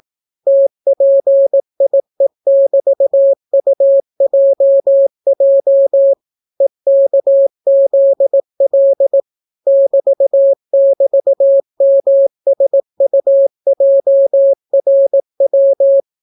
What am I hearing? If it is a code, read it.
Morse code, heard as a keyed tone, about 18 wpm, 550 Hz. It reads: TPIE<BT>UJJ EKZL <BT><BT>MSUJRW